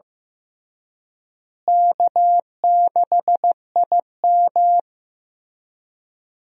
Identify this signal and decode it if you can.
Morse code, heard as a keyed tone, about 15 wpm, 700 Hz. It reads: K6IM
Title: USA Call Signs - 4 Characters - Repeated 3x - 15wpm